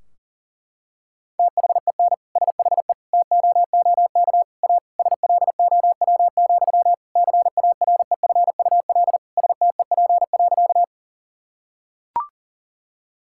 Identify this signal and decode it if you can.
Morse code, heard as a keyed tone, about 40 wpm, 700 Hz. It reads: THEN SHE TOOK A SLOW, CAREFUL STEP.
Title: Sentences from Top 1600 Words 40wpm